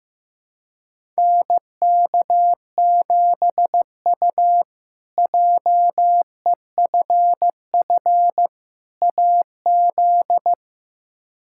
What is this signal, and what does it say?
Morse code, heard as a keyed tone, about 15 wpm, 700 Hz. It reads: NK7U JEFF AZ
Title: K1USN SST QSO Practice - 15wpm